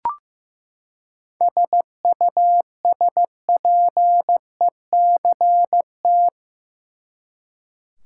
Morse code, heard as a keyed tone, about 15 wpm, 700 Hz. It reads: SUSPECT
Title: Seven Letter Words - 15wpm